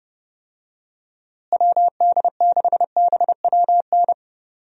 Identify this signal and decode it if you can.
Morse code, heard as a keyed tone, about 30 wpm, 700 Hz. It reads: WD6BWN
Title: USA Call Signs - 6 Characters - Repeated 2x - 30wpm